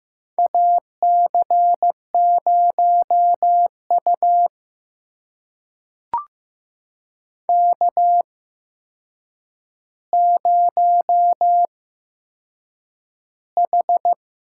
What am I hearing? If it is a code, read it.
Morse code, heard as a keyed tone, about 15 wpm, spaced as for 7 wpm, 700 Hz. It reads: AC0U K0H
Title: USA Call Signs - 4 Characters - 8x Character Spacing - 15wpm